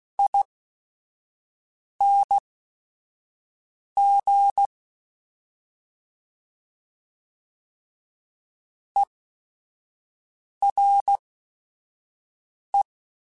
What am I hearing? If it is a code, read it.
Morse code, heard as a keyed tone, about 16 wpm, spaced as for 5 wpm, 800 Hz. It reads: ING ERE